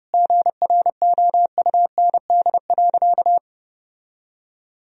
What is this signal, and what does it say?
Morse code, heard as a keyed tone, about 30 wpm, 700 Hz. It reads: GROUND.
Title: Sentences from Top 1000 Words - Review Part 2 - 30wpm